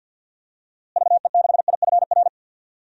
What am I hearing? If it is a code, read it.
Morse code, heard as a keyed tone, about 46 wpm, 700 Hz. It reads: VE6IFR
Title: DXCC-100 Call Signs - 6 Characters - Repeated 3x - 50wpm